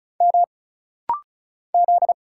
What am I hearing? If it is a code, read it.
Morse code, heard as a keyed tone, about 35 wpm, 700 Hz. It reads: M Z
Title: Single Letter-Number Alt - Rapid-Fire 35wpm